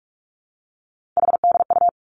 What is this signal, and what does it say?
Morse code, heard as a keyed tone, about 46 wpm, 700 Hz. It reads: HDU
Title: Three Challenging Letter-Number Combinations 45wpm